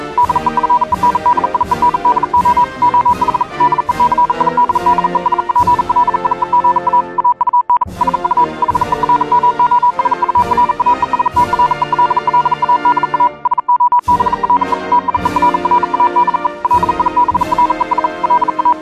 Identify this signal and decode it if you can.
Morse code, heard as a keyed tone, about 42 wpm, 980 Hz. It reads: DEFENDIENDO OH DECID DESPLIEGA AUN SU HERMOSURA ESTRELLADA SOBRE TIERRA DE LIBRES LA